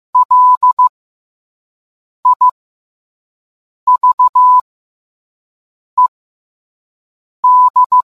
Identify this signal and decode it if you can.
Morse code, heard as a keyed tone, about 15 wpm, spaced as for 5 wpm, 1000 Hz. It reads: LIVED